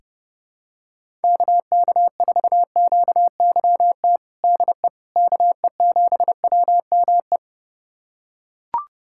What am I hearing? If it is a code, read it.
Morse code, heard as a keyed tone, about 29 wpm, 700 Hz. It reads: KK4QYT DE KE7WME
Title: USA Callsign DE Callsign - 6 Characters - 30wpm